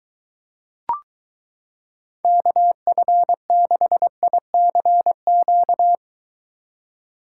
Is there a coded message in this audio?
Yes: Morse code from 2.2 to 6.0 s, heard as a keyed tone, about 23 wpm, 700 Hz. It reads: KF6ICQ